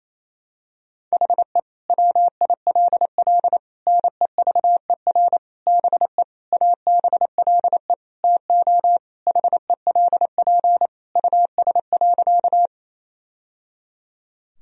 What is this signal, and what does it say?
Morse code, heard as a keyed tone, about 28 wpm, 700 Hz. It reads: HE WILL NEVER BE ABLE TO HELP US.
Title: Sentences from Top 1000 Words - Review Part 2 - 28wpm